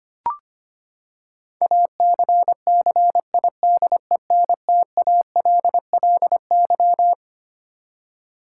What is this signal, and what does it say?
Morse code, heard as a keyed tone, about 25 wpm, 700 Hz. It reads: ACCIDENTALLY
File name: Twelve Letter Words - 25wpm